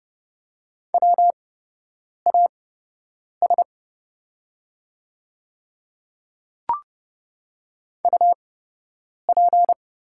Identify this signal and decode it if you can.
Morse code, heard as a keyed tone, about 30 wpm, spaced as for 8 wpm, 700 Hz. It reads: WAS UP
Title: 8x Character Spacing - Top 200 Words - 30wpm